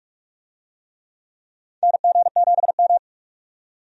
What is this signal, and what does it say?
Morse code, heard as a keyed tone, about 42 wpm, 700 Hz. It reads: NG7M
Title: USA Call Signs - 4 Characters - 45wpm